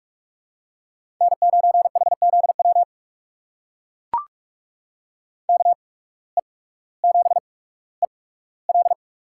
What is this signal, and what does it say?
Morse code, heard as a keyed tone, about 44 wpm, spaced as for 19 wpm, 700 Hz. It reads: N9HZW KE7EL
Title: USA Call Signs - 5 Characters - 8x Character Spacing - 45wpm